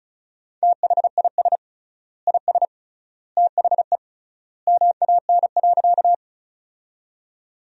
Morse code, heard as a keyed tone, about 34 wpm, 700 Hz. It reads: THIS IS THE MAN.
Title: 3x Word-Spacing - Sentences from Top 200 Words - 35wpm